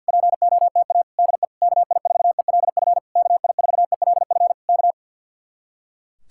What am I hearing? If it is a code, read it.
Morse code, heard as a keyed tone, about 64 wpm, 700 Hz. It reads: POTA DE KI4ELF KI4ELF K